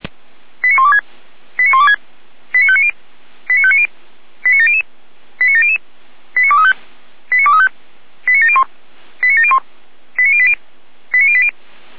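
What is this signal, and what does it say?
Five-tone selective call (ZVEI), heard as quick runs of five tones, tones of about 70 ms. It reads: 89126 89126 89590 89590 89700 89700 89235 89235 89701 89701 89080 89080
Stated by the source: Five-tone ZVEI selective call received on SDR.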